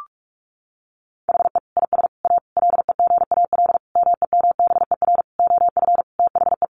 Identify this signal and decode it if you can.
Morse code, heard as a keyed tone, about 44 wpm, 700 Hz. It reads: HE IS A LEGAL MEMBER OF THE
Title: Sentences from Top 1200 Words 45wpm